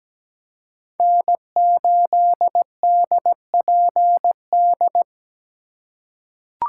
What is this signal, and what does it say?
Morse code, heard as a keyed tone, about 17 wpm, 700 Hz. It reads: N8DPD
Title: K1USN SST QSO Practice - 17wpm